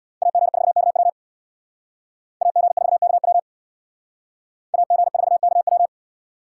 Morse code, heard as a keyed tone, about 64 wpm, 700 Hz. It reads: AG4KW AG4KW AG4KW